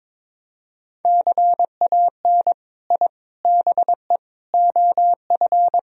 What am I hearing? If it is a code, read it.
Morse code, heard as a keyed tone, about 22 wpm, 700 Hz. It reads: CAN I BE OF
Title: Sentences from Top 1000 Words - Review Part 4 - 22wpm